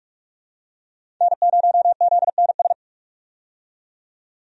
Morse code, heard as a keyed tone, about 46 wpm, 700 Hz. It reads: N0ZNS
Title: USA Call Signs - 5 Characters - 45wpm